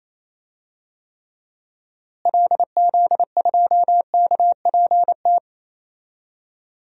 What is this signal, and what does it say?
Morse code, heard as a keyed tone, about 28 wpm, 700 Hz. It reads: LZ2KPT
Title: DXCC-100 Call Signs - 6 Characters - Repeated 3x - 28wpm